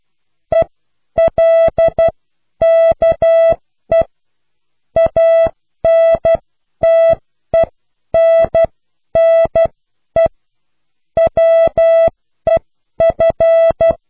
Morse code, heard as a keyed tone, about 12 wpm, spaced as for 10 wpm, 660 Hz. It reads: ELKE ANTENNE WEF